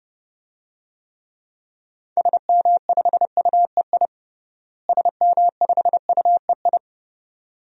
Morse code, heard as a keyed tone, about 32 wpm, 700 Hz. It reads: SM5UEI SM5UEI
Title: DXCC-100 Call Signs - 6 Characters - Repeated 2x - 30wpm